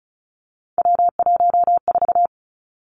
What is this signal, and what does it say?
Morse code, heard as a keyed tone, about 36 wpm, 700 Hz. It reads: W14
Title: Three Challenging Letter-Number Combinations 35wpm